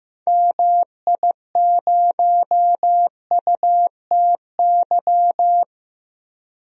Morse code, heard as a keyed tone, about 16 wpm, 700 Hz. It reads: MI0UTY
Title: DXCC-100 Call Signs - 6 Characters - 15wpm